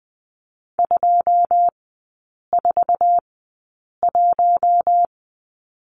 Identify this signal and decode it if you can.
Morse code, heard as a keyed tone, about 20 wpm, spaced as for 8 wpm, 700 Hz. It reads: 241
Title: 138 Lesson 23e - Letter f - Three Characters 20wpm